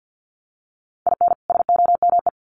Morse code, heard as a keyed tone, about 50 wpm, 700 Hz. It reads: IN SOME
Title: Sets of 2 Words 50wpm